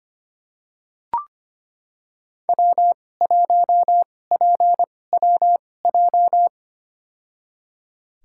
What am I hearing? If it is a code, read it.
Morse code, heard as a keyed tone, about 25 wpm, spaced as for 18 wpm, 700 Hz. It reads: W1PWJ
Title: USA Call Signs - 5 Characters - 2x Character Spacing - 25wpm